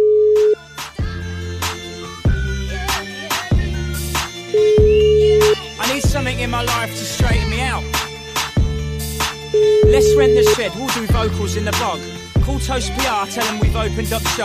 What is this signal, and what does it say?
Ringback tone (Europe, 425 Hz), with noise.